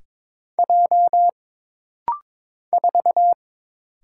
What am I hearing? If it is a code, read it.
Morse code, heard as a keyed tone, about 22 wpm, 700 Hz. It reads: J 4
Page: This rapid-fire timing is geared to push your ICR (Instant Character Recognition) skills.